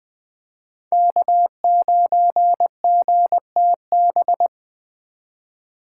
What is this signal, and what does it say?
Morse code, heard as a keyed tone, about 23 wpm, 700 Hz. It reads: K9GTB
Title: Speed Racing - USA 1x3 Call Signs 20wpm